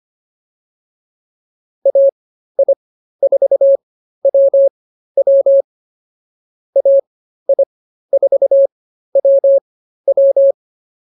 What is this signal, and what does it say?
Morse code, heard as a keyed tone, about 24 wpm, spaced as for 13 wpm, 550 Hz. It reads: AI4WW AI4WW